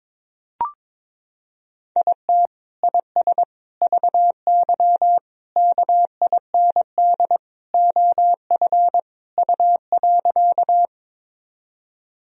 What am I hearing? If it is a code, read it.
Morse code, heard as a keyed tone, about 22 wpm, 700 Hz. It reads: IT IS VY KIND OF U.
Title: Sentences from Top 200 Words and Abbreviations 22wpm